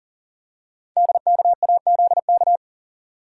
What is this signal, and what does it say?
Morse code, heard as a keyed tone, about 41 wpm, 700 Hz. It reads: DKAZK
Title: Code Groups - 5 Letters - 40wpm